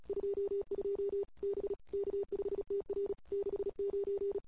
Morse code, heard as a keyed tone, about 35 wpm, 410 Hz. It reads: 22 B K5TR 69
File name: CW / 2008-ss-cw-k5tr-op-n5rz -21:00